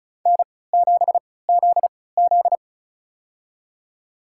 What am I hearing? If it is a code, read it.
Morse code, heard as a keyed tone, about 36 wpm, spaced as for 20 wpm, 690 Hz. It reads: N7ZZ
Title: USA Call Signs - 4 Characters - 3x Character Spacing - 35wpm